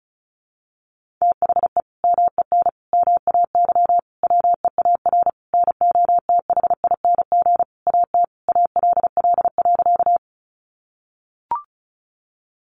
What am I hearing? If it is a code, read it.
Morse code, heard as a keyed tone, about 34 wpm, 700 Hz. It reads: THE MEN MAY WEAR NOTHING AT ALL.
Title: Sentences from Top 1500 Words 35wpm